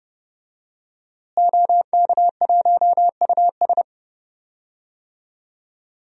Morse code, heard as a keyed tone, about 31 wpm, 700 Hz. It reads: OK1US
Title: DXCC-100 Call Signs - 5 Characters - Repeated 3x - 30wpm